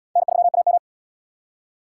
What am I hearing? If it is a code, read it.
Morse code, heard as a keyed tone, about 74 wpm, 700 Hz. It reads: N3AM